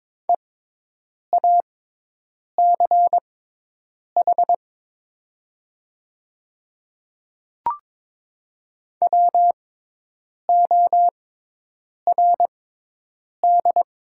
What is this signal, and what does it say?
Morse code, heard as a keyed tone, about 22 wpm, spaced as for 8 wpm, 700 Hz. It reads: EACH WORD
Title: 6x Character Spacing - Top 200 Words - 22wpm